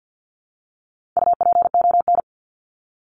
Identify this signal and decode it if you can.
Morse code, heard as a keyed tone, about 50 wpm, 700 Hz. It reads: UPON
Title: Top 1001-1250 Words 50wpm